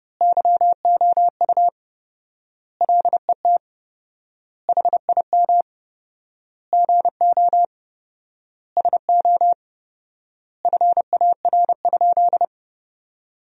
Morse code, heard as a keyed tone, about 31 wpm, 700 Hz. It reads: YOU LET HIM GO SO FAR?
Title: 4x Word-Spacing - Sentences from Top 300 Words - 30wpm